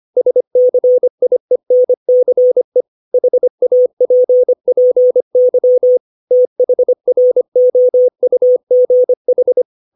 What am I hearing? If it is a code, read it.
Morse code, heard as a keyed tone, about 25 wpm, 500 Hz. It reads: SCIENCE HAPPY THROUGH
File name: 632 words / 632 Words 25WPM